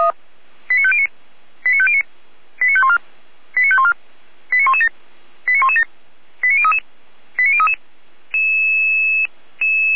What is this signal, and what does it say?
Five-tone selective call (ZVEI), heard as quick runs of five tones, tones of about 80 ms. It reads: 89509 89509 89514 89514 89117 89117 89033 89033